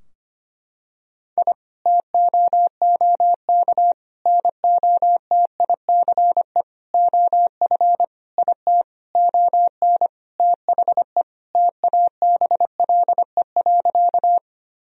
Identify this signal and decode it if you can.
Morse code, heard as a keyed tone, about 25 wpm, 700 Hz. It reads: I TOOK NOTICE OF IT ON THE TABLE.
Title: Sentences from Top 1000 Words - Review Part 2 - 25wpm